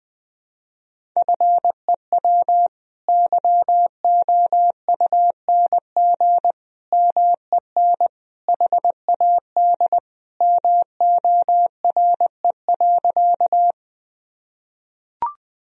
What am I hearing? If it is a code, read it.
Morse code, heard as a keyed tone, about 21 wpm, 700 Hz. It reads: FEW YOUNG MEN HAD MORE.
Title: Sentences from Top 1000 Words - Review Part 1 - 20wpm